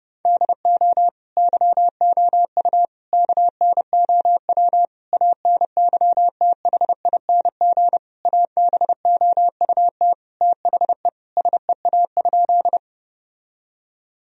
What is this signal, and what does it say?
Morse code, heard as a keyed tone, about 30 wpm, 700 Hz. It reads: DO YOU KNOW ANYTHING ABOUT THE SEA?
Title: Sentences from Top 1000 Words - Review Part 1 - 30wpm